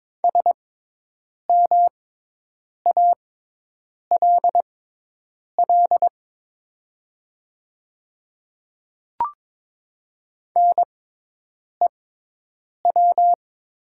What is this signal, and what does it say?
Morse code, heard as a keyed tone, about 22 wpm, spaced as for 8 wpm, 700 Hz. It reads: SMALL NEW
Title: 6x Character Spacing - Top 300 Words - 22wpm